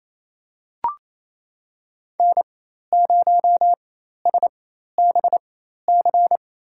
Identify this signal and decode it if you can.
Morse code, heard as a keyed tone, about 28 wpm, spaced as for 13 wpm, 700 Hz. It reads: N0SBC
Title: USA Call Signs - 5 Characters - 4x Character Spacing - 28wpm